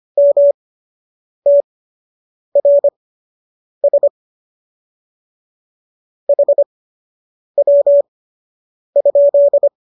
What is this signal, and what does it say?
Morse code, heard as a keyed tone, about 25 wpm, spaced as for 8 wpm, 570 Hz. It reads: MTRS HW?